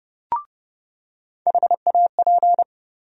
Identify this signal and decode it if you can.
Morse code, heard as a keyed tone, about 32 wpm, 700 Hz. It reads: HAP
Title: USA Call Sign Postfixes - 3 Letters 30wpm